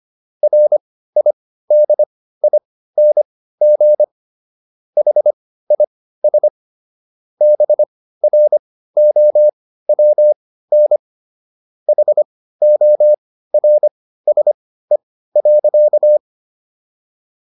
Morse code, heard as a keyed tone, about 25 wpm, spaced as for 15 wpm, 600 Hz. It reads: RIDING HIS BROWN HORSE.